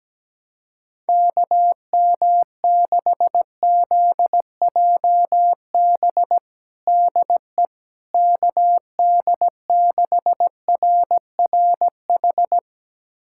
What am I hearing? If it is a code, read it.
Morse code, heard as a keyed tone, about 17 wpm, 700 Hz. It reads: KM6ZJB DE KD6RRH
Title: USA Callsign DE Callsign 17wpm